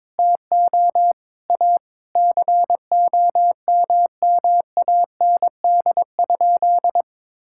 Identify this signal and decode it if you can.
Morse code, heard as a keyed tone, about 22 wpm, 700 Hz. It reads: TO A COMMAND?
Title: Sentences from Top 1600 Words 22wpm